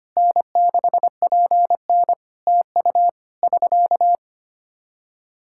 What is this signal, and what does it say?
Morse code, heard as a keyed tone, about 25 wpm, 700 Hz. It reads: N6PN TU <SK>
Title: Practice QSO - Exchange 4 - Ending - 25wpm